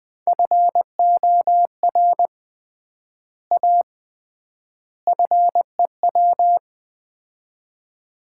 Morse code, heard as a keyed tone, about 21 wpm, 700 Hz. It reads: FOR A FEW
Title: 3x Word-Spacing - Sets of 3 Words - 20wpm